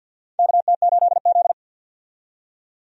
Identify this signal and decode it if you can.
Morse code, heard as a keyed tone, about 53 wpm, 700 Hz. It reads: KT8Z
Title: USA Call Signs - 4 Characters - 50wpm